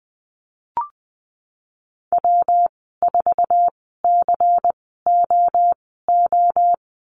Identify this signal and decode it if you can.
Morse code, heard as a keyed tone, about 20 wpm, spaced as for 14 wpm, 700 Hz. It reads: W4COO
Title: 122 Lesson 20g - Number 3 - Call Signs 20wpm